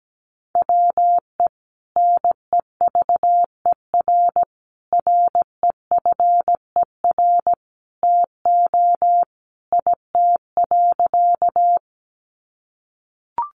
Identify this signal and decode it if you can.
Morse code, heard as a keyed tone, about 17 wpm, 700 Hz. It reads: WE NEVER REFER TO IT.
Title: Sentences from Top 1000 Words 17wpm